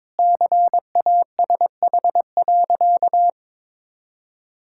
Morse code, heard as a keyed tone, about 21 wpm, 700 Hz. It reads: CASH.
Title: Sentences from Top 1000 Words - Review Part 5 - 22wpm